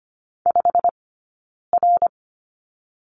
Morse code, heard as a keyed tone, about 24 wpm, spaced as for 9 wpm, 700 Hz. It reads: 5R
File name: Lesson 15d - Number 5 - Two Characters 25wpm